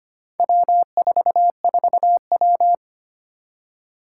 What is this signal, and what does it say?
Morse code, heard as a keyed tone, about 25 wpm, 700 Hz. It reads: W44W
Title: Four Challenging Letter-Number Combinations 25wpm